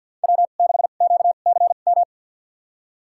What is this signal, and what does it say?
Morse code, heard as a keyed tone, about 47 wpm, spaced as for 35 wpm, 690 Hz. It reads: W6QCM
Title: USA Call Signs - 2x Character Spacing - 50wpm